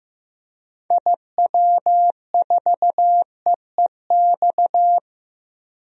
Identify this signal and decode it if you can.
Morse code, heard as a keyed tone, about 15 wpm, 700 Hz. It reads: IW4EEX
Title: DXCC-100 Call Signs - 6 Characters - 15wpm